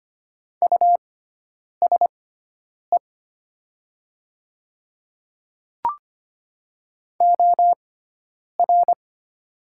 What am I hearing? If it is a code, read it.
Morse code, heard as a keyed tone, about 25 wpm, spaced as for 9 wpm, 700 Hz. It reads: USE OR
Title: 6x Character Spacing - Top 100 Words - 25wpm